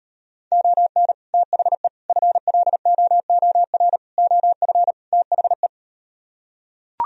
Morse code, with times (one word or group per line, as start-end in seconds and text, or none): 0.5-1.1 ON
1.3-1.9 THE
2.1-4.0 FLOOR
4.2-4.9 OF
5.1-5.7 THE